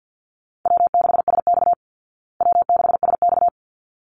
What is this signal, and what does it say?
Morse code, heard as a keyed tone, about 50 wpm, 700 Hz. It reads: W6SX W6SX
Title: Call Signs Repeated 3x - 50wpm